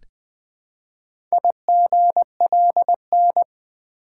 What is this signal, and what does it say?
Morse code, heard as a keyed tone, about 20 wpm, 700 Hz. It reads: IGLN
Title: Four Letter-Number Combinations - 20wpm